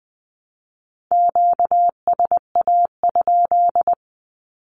Morse code, heard as a keyed tone, about 20 wpm, 700 Hz. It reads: QSA?
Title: Q-Codes Repeated 3x 20wpm